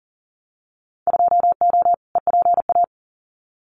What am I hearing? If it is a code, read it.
Morse code, heard as a keyed tone, about 40 wpm, 700 Hz. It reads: 2O EPA
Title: Winter Field Day Exchanges Encore 40wpm